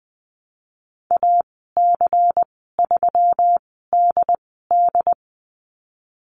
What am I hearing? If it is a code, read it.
Morse code, heard as a keyed tone, about 20 wpm, 700 Hz. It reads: A C 3 D D
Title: 154 Lesson 25g - Letter p - Call Signs 20wpm